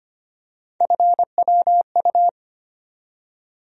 Morse code, heard as a keyed tone, about 25 wpm, 700 Hz. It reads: FWU
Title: USA Call Sign Postfixes - 3 Letters 25wpm